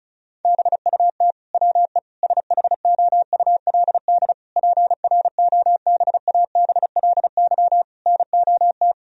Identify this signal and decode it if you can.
Morse code, heard as a keyed tone, about 36 wpm, 700 Hz. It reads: BUT WE SHOULD PROBABLY NOT
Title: Sentences from Top 1000 Words - Review Part 4 - 35wpm